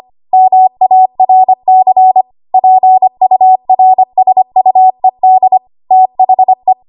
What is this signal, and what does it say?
Morse code, heard as a keyed tone, about 25 wpm, 750 Hz. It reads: MARC PURSUED THE
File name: Morse / 25WPM